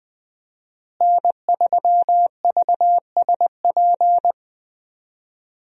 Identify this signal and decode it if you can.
Morse code, heard as a keyed tone, about 20 wpm, 700 Hz. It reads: N3VSP
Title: USA Call Signs - 5 Characters - Repeated 2x - 20wpm